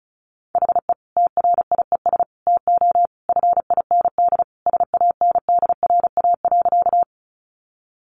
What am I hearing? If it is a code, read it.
Morse code, heard as a keyed tone, about 34 wpm, 700 Hz. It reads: HE TRIES TO FIND SANDRA.
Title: Easy Phrases 35wpm